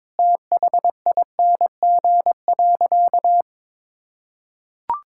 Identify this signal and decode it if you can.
Morse code, heard as a keyed tone, about 21 wpm, 700 Hz. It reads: THING.
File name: practice / 22wpm / Sentences with 3-Word Phrases - 22wpm